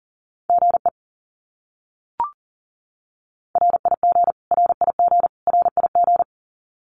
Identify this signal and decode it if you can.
Morse code, heard as a keyed tone, about 40 wpm, 700 Hz. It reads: GE RIG RIG RIG
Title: Abbreviations Repeated 3x - 40wpm